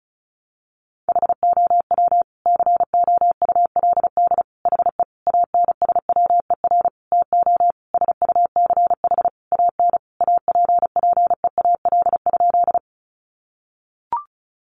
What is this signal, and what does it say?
Morse code, heard as a keyed tone, about 34 wpm, 700 Hz. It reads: HOW COULD HE ANSWER TO SUCH AN APPEAL?
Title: Sentences from Top 1300 Words 35wpm